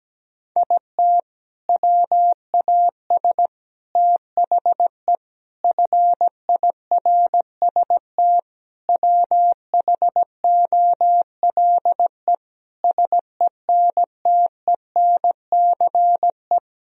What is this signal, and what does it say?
Morse code, heard as a keyed tone, about 17 wpm, 700 Hz. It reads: IT WAS THE FIRST WHOLE SENTENCE
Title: Sentences from Top 1000 Words - Review Part 1 - 17wpm